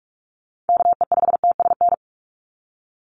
Morse code, heard as a keyed tone, about 44 wpm, 700 Hz. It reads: KE5TSN
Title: USA Call Signs 45wpm